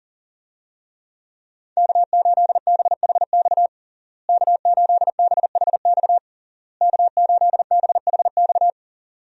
Morse code, heard as a keyed tone, about 40 wpm, 700 Hz. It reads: K8BHX K8BHX K8BHX
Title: USA Call Signs - 5 Characters - Repeated 3x - 40wpm